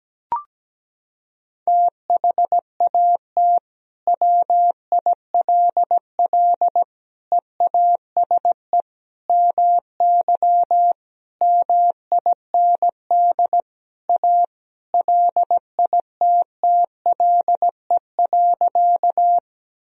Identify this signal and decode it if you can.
Morse code, heard as a keyed tone, about 17 wpm, 700 Hz. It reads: THAT WILL EASE MY MIND A LITTLE.
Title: Sentences from Top 1000 Words - Review Part 1 - 17wpm